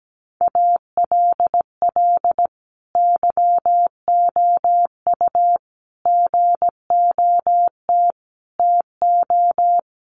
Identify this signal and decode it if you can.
Morse code, heard as a keyed tone, about 17 wpm, 700 Hz. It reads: ALL YOU GOT TO
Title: A Sound Of Thunder 17wpm